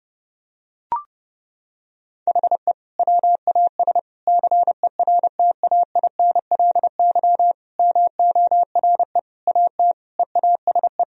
Morse code, heard as a keyed tone, about 32 wpm, 700 Hz. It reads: HE WAS CERTAINLY MORE AT EASE
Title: Sentences from Top 1000 Words - Review Part 1 - 30wpm